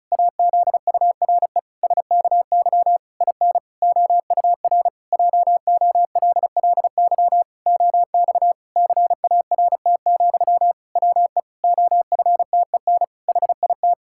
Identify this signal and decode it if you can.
Morse code, heard as a keyed tone, about 36 wpm, 700 Hz. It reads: AZURE SKY IN OUR JOLLY OX CART, WE OFTEN HIT
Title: Pangram Challenge v2 - 35wpm